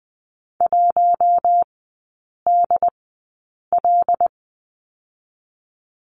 Morse code, heard as a keyed tone, about 20 wpm, spaced as for 8 wpm, 700 Hz. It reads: 1DL
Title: 100 Lesson 17e - Letter u - Three Characters 20wpm